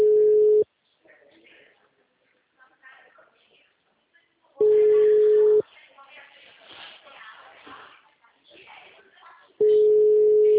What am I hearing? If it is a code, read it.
Ringback tone (Europe, 425 Hz), with speech.
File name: out- -2006- - - .1